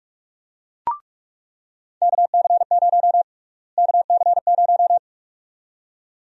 Morse code, heard as a keyed tone, about 46 wpm, 700 Hz. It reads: KC0 KC0
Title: USA Call Sign Prefixes - Repeated 2x - 45wpm